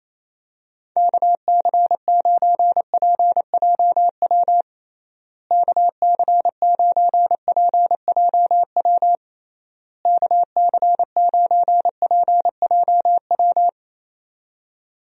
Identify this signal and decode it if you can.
Morse code, heard as a keyed tone, about 27 wpm, 700 Hz. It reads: KC9PJW KC9PJW KC9PJW
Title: USA Call Signs - 6 Characters - Repeated 3x - 28wpm